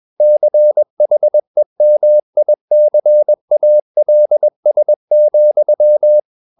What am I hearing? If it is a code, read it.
Morse code, heard as a keyed tone, about 21 wpm, 600 Hz. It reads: CHEMICALS,
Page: CW med Gnister